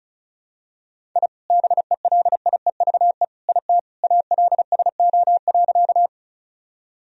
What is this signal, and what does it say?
Morse code, heard as a keyed tone, about 36 wpm, 700 Hz. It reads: I BELIEVE IT ALSO.
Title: Sentences from Top 1000 Words - Review Part 1 - 35wpm